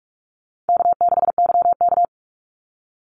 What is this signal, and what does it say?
Morse code, heard as a keyed tone, about 44 wpm, 700 Hz. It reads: K6YK
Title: NA SOTA Chaser Callsigns 45wpm